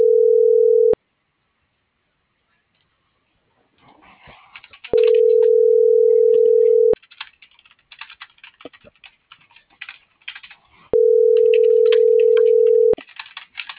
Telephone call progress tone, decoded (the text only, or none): ringback tone